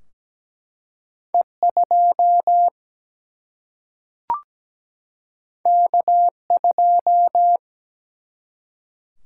Morse code, heard as a keyed tone, about 17 wpm, 700 Hz. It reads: E 2 K 2
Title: Single-Letter Single-Number 17wpm